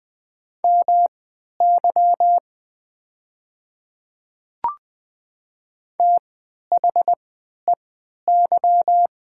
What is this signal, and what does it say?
Morse code, heard as a keyed tone, about 20 wpm, spaced as for 11 wpm, 700 Hz. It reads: MY THEY
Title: 3x Character Spacing - Top 100 Words - 20wpm